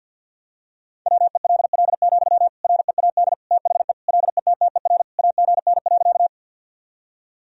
Morse code, heard as a keyed tone, about 51 wpm, 700 Hz. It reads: WELL, READ THE LETTER AGN.
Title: Sentences from Top 300 Words and Abbreviations 50wpm